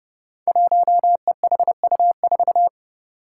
Morse code, heard as a keyed tone, about 28 wpm, 700 Hz. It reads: 1EHU4
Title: Five Challenging Letter-Number Combinations 30wpm